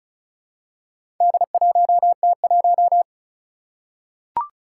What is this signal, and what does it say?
Morse code, heard as a keyed tone, about 34 wpm, 700 Hz. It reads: D1T1
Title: Four Challenging Letter-Number Combinations 35wpm